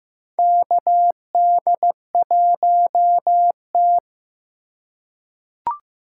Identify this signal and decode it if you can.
Morse code, heard as a keyed tone, about 15 wpm, 700 Hz. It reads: KD1T
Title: USA Call Signs - 4 Characters - 3x Character Spacing - 15wpm